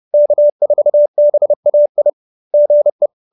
Morse code, heard as a keyed tone, about 30 wpm, 590 Hz. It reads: K4BAI GE